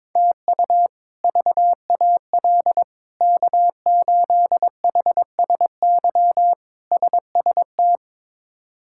Morse code, heard as a keyed tone, about 22 wpm, 700 Hz. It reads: TU VAL K8HSY SST
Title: K1USN SST QSO Practice - 22wpm